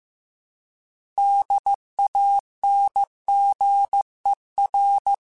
Morse code, heard as a keyed tone, about 15 wpm, 800 Hz. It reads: DANGER
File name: fromSMs / FromSMs mp3s / Cities@13